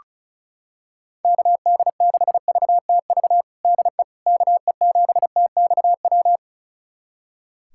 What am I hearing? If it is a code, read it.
Morse code, heard as a keyed tone, about 34 wpm, 700 Hz. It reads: KD6VTV DE KE7TXW
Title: USA Callsign DE Callsign 35wpm